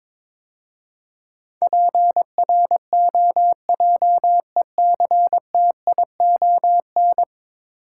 Morse code, heard as a keyed tone, about 22 wpm, 700 Hz. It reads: PROJECTION
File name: practice / 22wpm / 10 Letter Words - First 3 Letters Given - 22wpm